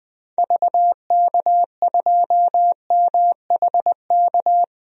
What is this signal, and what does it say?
Morse code, heard as a keyed tone, about 20 wpm, 700 Hz. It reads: VK2MHK
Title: DXCC-100 Call Signs - 6 Characters - Repeated 2x - 20wpm